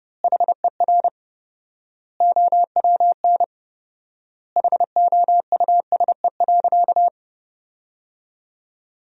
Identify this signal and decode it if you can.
Morse code, heard as a keyed tone, about 30 wpm, 700 Hz. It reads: HER OWN HOUSE.
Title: 4x Word-Spacing - Sentences from Top 300 Words - 30wpm